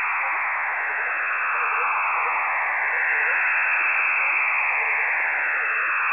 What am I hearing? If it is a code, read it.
Emergency Alert System (SAME) tones (header bursts), heard as harsh buzzing bursts. It is heard from 0.6 to 5.7 s.